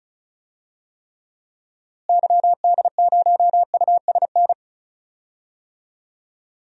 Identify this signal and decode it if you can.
Morse code, heard as a keyed tone, about 36 wpm, 700 Hz. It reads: YD0USN
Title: DXCC-100 Call Signs - 6 Characters - Repeated 3x - 35wpm